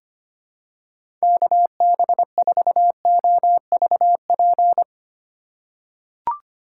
Morse code, heard as a keyed tone, about 25 wpm, 700 Hz. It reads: KB4OVP
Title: USA Call Signs - 6 Characters - 4x Character Spacing - 25wpm